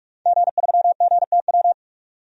Morse code, heard as a keyed tone, about 46 wpm, 700 Hz. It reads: G3GTW
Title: DXCC-100 Call Signs - 5 Characters - 45wpm